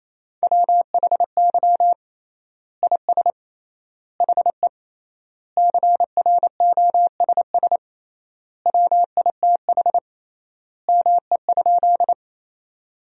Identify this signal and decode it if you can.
Morse code, heard as a keyed tone, about 28 wpm, 700 Hz. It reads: WHY IS HE CROSS WITH ME?
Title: 3x Word-Spacing - Sentences from Top 300 Words - 28wpm